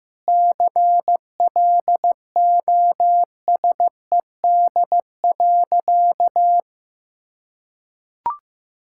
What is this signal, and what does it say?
Morse code, heard as a keyed tone, about 15 wpm, 700 Hz. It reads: CLOSED.
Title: Sentences from Top 1600 Words 15wpm